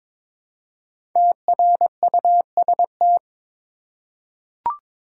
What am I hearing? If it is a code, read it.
Morse code, heard as a keyed tone, about 22 wpm, 700 Hz. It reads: TRUST
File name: 5 Letter Words - First 3 Letters Given - 22wpm